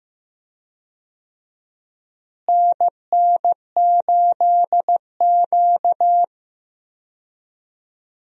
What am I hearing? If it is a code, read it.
Morse code, heard as a keyed tone, about 16 wpm, 700 Hz. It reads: NN8Q
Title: USA Call Signs - 4 Characters - Repeated 2x - 15wpm